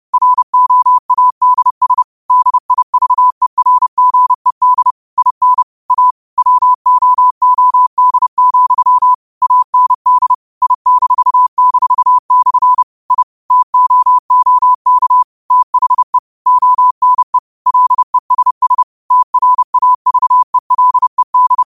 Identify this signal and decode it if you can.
Morse code, heard as a keyed tone, about 30 wpm, 1000 Hz. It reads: ROADS DIVERGED IN A WOOD, AND I--/ I TOOK THE ONE LESS TRAVELED
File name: qotdfiles / 30wpm